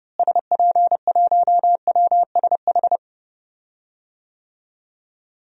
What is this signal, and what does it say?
Morse code, heard as a keyed tone, about 31 wpm, 700 Hz. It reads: SP1WSH
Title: DXCC-100 Call Signs - 6 Characters - Repeated 3x - 30wpm